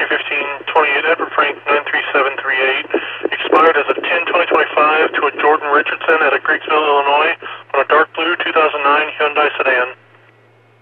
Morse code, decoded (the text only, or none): none